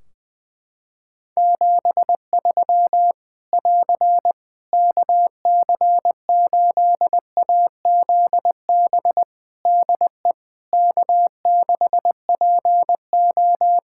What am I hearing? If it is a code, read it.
Morse code, heard as a keyed tone, about 20 wpm, 700 Hz. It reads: 73 <AR> KC8AZB DE K6PO
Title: Practice QSO - Exchange 4 - Ending - 20wpm